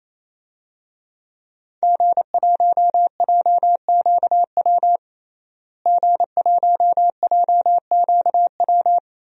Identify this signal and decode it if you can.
Morse code, heard as a keyed tone, about 28 wpm, 700 Hz. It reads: G1JQW G1JQW
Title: DXCC-100 Call Signs - 5 Characters - Repeated 3x - 28wpm